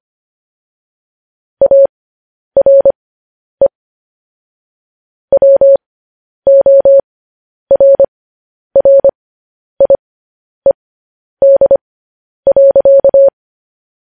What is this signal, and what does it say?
Morse code, heard as a keyed tone, about 25 wpm, spaced as for 10 wpm, 550 Hz. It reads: ARE WORRIED.